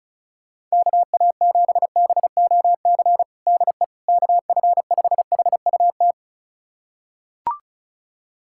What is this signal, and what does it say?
Morse code, heard as a keyed tone, about 34 wpm, 700 Hz. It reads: KA7BOC DE KF5HUT
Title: USA Callsign DE Callsign - 6 Characters - 35wpm